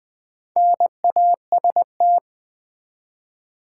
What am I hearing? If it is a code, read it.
Morse code, heard as a keyed tone, about 20 wpm, 700 Hz. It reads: NAST
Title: Build A Word - 1 Letter At A Time - 5-Letter Words - 20wpm